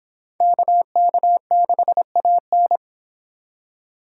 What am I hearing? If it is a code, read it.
Morse code, heard as a keyed tone, about 29 wpm, 700 Hz. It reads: KK6AN